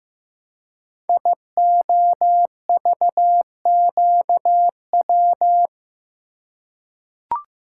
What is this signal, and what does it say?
Morse code, heard as a keyed tone, about 15 wpm, 700 Hz. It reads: IOVQW
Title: Code Groups - 5 Letter:Numbers - 15wpm